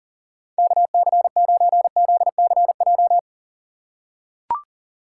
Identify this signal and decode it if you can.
Morse code, heard as a keyed tone, about 40 wpm, 700 Hz. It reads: KC9ZCJ
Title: USA Call Signs - 3x Character Spacing - 40wpm